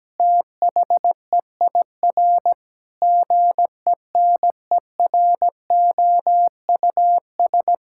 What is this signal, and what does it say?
Morse code, heard as a keyed tone, about 17 wpm, 700 Hz. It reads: THEIR GENEROUS
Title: Pangram Challenge v2 - 17wpm